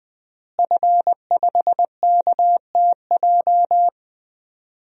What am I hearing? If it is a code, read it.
Morse code, heard as a keyed tone, about 20 wpm, 700 Hz. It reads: F5KTJ
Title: DXCC-100 Call Signs - 5 Characters - Repeated 3x - 20wpm